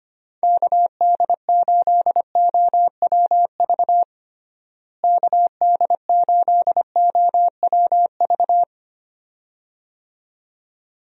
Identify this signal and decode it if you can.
Morse code, heard as a keyed tone, about 25 wpm, 700 Hz. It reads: KD8OWV KD8OWV
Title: USA Call Signs - 6 Characters - Repeated 2x - 25wpm